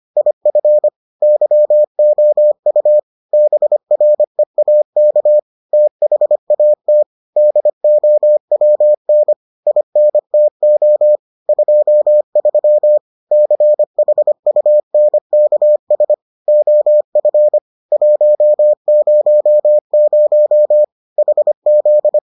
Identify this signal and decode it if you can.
Morse code, heard as a keyed tone, about 25 wpm, 600 Hz. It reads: IF YOU BREAK THAT DOWN INTO 23 CHUNKS OF 100 HZ